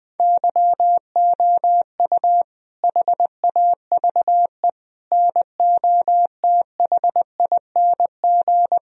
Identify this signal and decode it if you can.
Morse code, heard as a keyed tone, about 19 wpm, 700 Hz. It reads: YOU HAVE NOTHING
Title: Sentences from Top 1000 Words - Review Part 1 - 20wpm